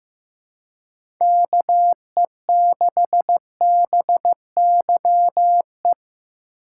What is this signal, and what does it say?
Morse code, heard as a keyed tone, about 15 wpm, 700 Hz. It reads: KE6BYE
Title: USA Call Signs - 6 Characters - 4x Character Spacing - 15wpm